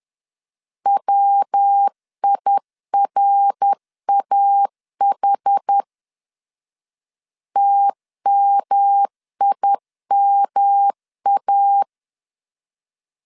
Morse code, heard as a keyed tone, about 10 wpm, 800 Hz. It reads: WIRAH TMIMA
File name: CW / DET / 1 3 1500C S B 10wpm 3335